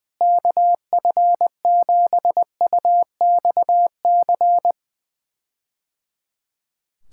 Morse code, heard as a keyed tone, about 20 wpm, 700 Hz. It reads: KF7UXC
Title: K1USN SST QSO Practice - 20wpm